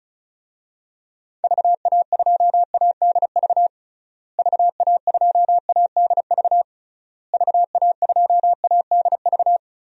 Morse code, heard as a keyed tone, about 34 wpm, 700 Hz. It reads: VA2ADV VA2ADV VA2ADV
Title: DXCC-100 Call Signs - 6 Characters - Repeated 3x - 35wpm